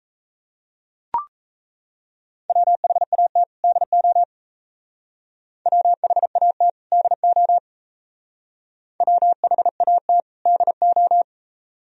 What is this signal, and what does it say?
Morse code, heard as a keyed tone, about 35 wpm, 700 Hz. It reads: WHAT DO WHAT DO WHAT DO